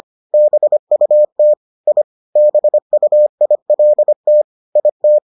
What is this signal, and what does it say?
Morse code, heard as a keyed tone, about 25 wpm, 600 Hz. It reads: BUT I BUILT IT